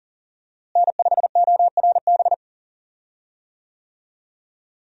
Morse code, heard as a keyed tone, about 40 wpm, 700 Hz. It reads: N5ORB
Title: USA Call Signs - 5 Characters - Repeated 2x - 40wpm